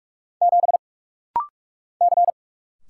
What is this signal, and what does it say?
Morse code, heard as a keyed tone, about 44 wpm, 700 Hz. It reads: 7 C
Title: Single Letter-Number - Rapid-Fire 45wpm